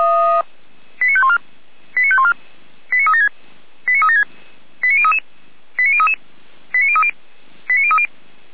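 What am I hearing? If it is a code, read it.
Five-tone selective call (ZVEI), heard as quick runs of five tones, tones of about 70 ms. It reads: 89514 89514 89376 89376 89033 89033 89030 89030
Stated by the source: ZVEI five-tone selective call received by SDR, recorded by fire brigade feuerwehr-orlamuende, channel 455.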